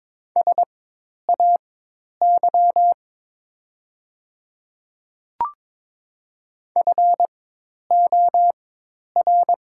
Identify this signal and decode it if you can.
Morse code, heard as a keyed tone, about 22 wpm, spaced as for 10 wpm, 700 Hz. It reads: SAY FOR
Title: 4x Character Spacing - Top 300 Words - 22wpm